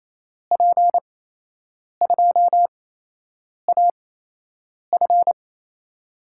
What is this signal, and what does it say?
Morse code, heard as a keyed tone, about 28 wpm, spaced as for 8 wpm, 700 Hz. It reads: P2AF
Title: USA Call Signs - 5 Characters - 8x Character Spacing - 28wpm